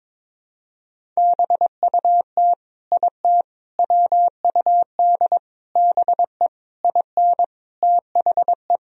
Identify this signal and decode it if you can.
Morse code, heard as a keyed tone, about 22 wpm, 700 Hz. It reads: BUT IT WUD BE IN THE
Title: Sentences from Top 300 Words and Abbreviations 22wpm